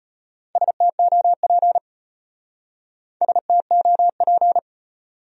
Morse code, heard as a keyed tone, about 30 wpm, 700 Hz. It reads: STOP STOP
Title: Speed-Racing Top 300 Words 25wpm